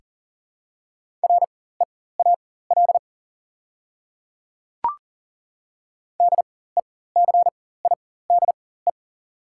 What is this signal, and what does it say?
Morse code, heard as a keyed tone, about 40 wpm, spaced as for 19 wpm, 700 Hz. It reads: REAL DECIDE
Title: 4x Character Spacing - Top 500 Words - 40wpm